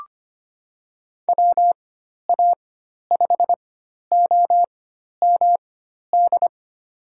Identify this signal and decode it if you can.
Morse code, heard as a keyed tone, about 25 wpm, spaced as for 12 wpm, 700 Hz. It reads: WA5OMD
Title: USA Call Signs - 4x Character Spacing - 25wpm